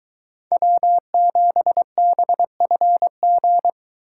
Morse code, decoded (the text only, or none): W7BFG